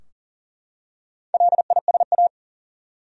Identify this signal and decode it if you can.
Morse code, heard as a keyed tone, about 40 wpm, 700 Hz. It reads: LISA
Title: USA Common Names - Max 4 Letters - 40wpm